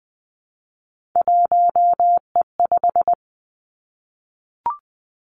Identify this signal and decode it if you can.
Morse code, heard as a keyed tone, about 20 wpm, 700 Hz. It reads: 1E5
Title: Three Challenging Letter-Number Combinations 20wpm